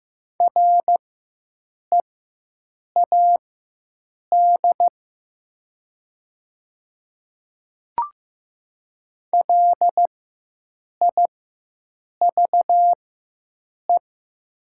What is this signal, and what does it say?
Morse code, heard as a keyed tone, about 15 wpm, spaced as for 7 wpm, 700 Hz. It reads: READ LIVE
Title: 4x Character Spacing - Top 300 Words - 15wpm